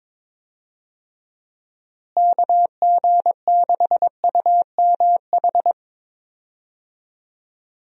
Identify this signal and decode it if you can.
Morse code, heard as a keyed tone, about 22 wpm, 700 Hz. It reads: KG6UMH
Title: USA Call Signs - 6 Characters - Repeated 2x - 22wpm